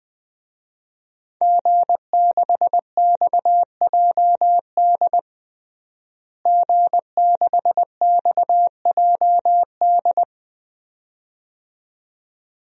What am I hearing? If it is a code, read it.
Morse code, heard as a keyed tone, about 20 wpm, 700 Hz. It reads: G6XJD G6XJD
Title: DXCC-100 Call Signs - 5 Characters - Repeated 2x - 20wpm